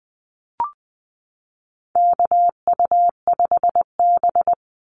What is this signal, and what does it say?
Morse code, heard as a keyed tone, about 20 wpm, 700 Hz. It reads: KU5B
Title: Call Signs Repeated 2x - 20wpm